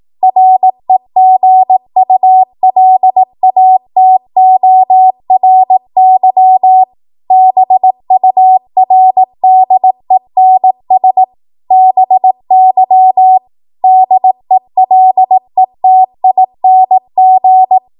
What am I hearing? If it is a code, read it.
Morse code, heard as a keyed tone, about 18 wpm, 750 Hz. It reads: REGULATORY BURDENS BY DELETING
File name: Morse / 18WPM